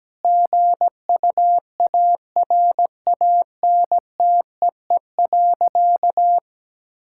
Morse code, heard as a keyed tone, about 17 wpm, 700 Hz. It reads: GUARANTEE.
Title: Sentences from Top 1600 Words 17wpm